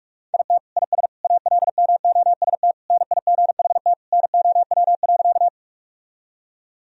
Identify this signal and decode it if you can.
Morse code, heard as a keyed tone, about 46 wpm, 700 Hz. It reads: IT IS ALMOST NIGHT NOW.
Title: Sentences from Top 1000 Words - Review Part 4 - 45wpm